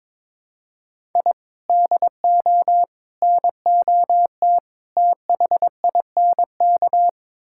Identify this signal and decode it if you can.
Morse code, heard as a keyed tone, about 22 wpm, 700 Hz. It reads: I DO NOT THINK
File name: practice / 22wpm / Build A Phrase - 5-Word Phrases - 22wpm